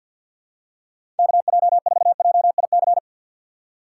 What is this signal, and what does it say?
Morse code, heard as a keyed tone, about 52 wpm, 700 Hz. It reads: KJ4JIC
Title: USA Call Signs - 6 Characters - 3x Character Spacing - 50wpm